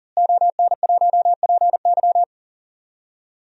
Morse code, heard as a keyed tone, about 40 wpm, 700 Hz. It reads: ON1PY
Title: DXCC-100 Call Signs - 5 Characters - Repeated 2x - 40wpm